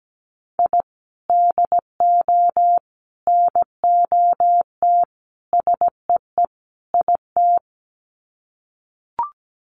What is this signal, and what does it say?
Morse code, heard as a keyed tone, about 17 wpm, 700 Hz. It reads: I DO NOT SEE IT
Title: Sets of 5 Words Encore2 17wpm